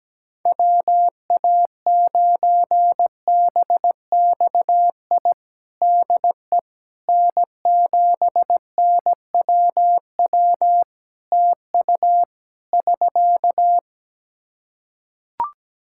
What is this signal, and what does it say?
Morse code, heard as a keyed tone, about 17 wpm, 700 Hz. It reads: WA9BXI DE N7NWW TU <SK>
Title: Practice QSO - Exchange 4 - Ending - 17wpm